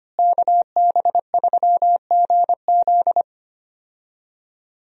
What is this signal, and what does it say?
Morse code, heard as a keyed tone, about 25 wpm, 700 Hz. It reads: KB3GZ
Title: USA Call Signs - 5 Characters - Repeated 2x - 25wpm